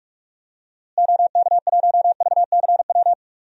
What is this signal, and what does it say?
Morse code, heard as a keyed tone, about 44 wpm, 700 Hz. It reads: OK1VCW
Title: DXCC-100 Call Signs - 6 Characters - Repeated 3x - 45wpm